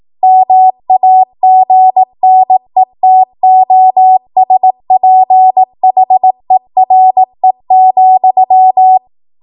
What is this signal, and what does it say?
Morse code, heard as a keyed tone, about 18 wpm, 750 Hz. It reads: MAGNETOSPHERE,
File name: Morse / 18WPM